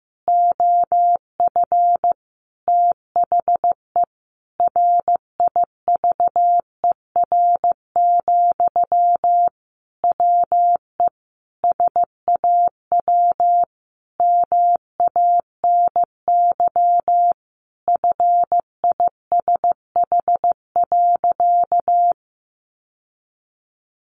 Morse code, heard as a keyed tone, about 15 wpm, 700 Hz. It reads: OF THE RIVER, WE SAW MANY FISH.
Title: Sentences from Top 1500 Words 15wpm